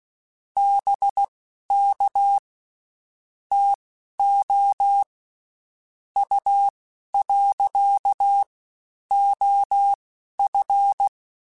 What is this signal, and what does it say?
Morse code, heard as a keyed tone, about 16 wpm, spaced as for 11 wpm, 800 Hz. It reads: BK TO U. OF
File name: (spaced) / QSOs Spaced-mp3s / WA7QOX 1 @15 sp